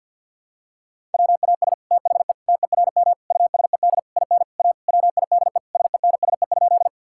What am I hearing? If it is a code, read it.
Morse code, heard as a keyed tone, about 49 wpm, 700 Hz. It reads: WAS THE TERM USED IN A WIDE SENSE?
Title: Sentences from Top 1000 Words - Review Part 6 - 50wpm